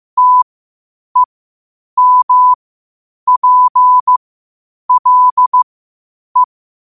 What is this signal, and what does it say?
Morse code, heard as a keyed tone, about 15 wpm, spaced as for 8 wpm, 1000 Hz. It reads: TEMPLE
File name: qotdfiles / 8wpm